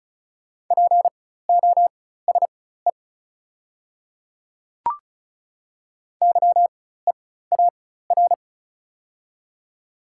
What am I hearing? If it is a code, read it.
Morse code, heard as a keyed tone, about 34 wpm, spaced as for 16 wpm, 700 Hz. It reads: POSE YEAR
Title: 4x Character Spacing - Top 500 Words - 35wpm